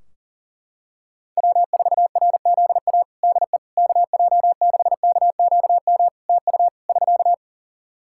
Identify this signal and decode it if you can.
Morse code, heard as a keyed tone, about 37 wpm, 700 Hz. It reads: W4RZA DE KJ6KQM TU <SK>
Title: Practice QSO - Exchange 4 - Ending - 40wpm